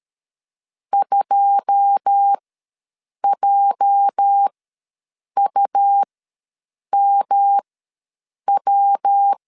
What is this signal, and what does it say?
Morse code, heard as a keyed tone, about 13 wpm, spaced as for 7 wpm, 800 Hz. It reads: 2JUMW
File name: LEZIONI / 9 2 ws 12wpm 3406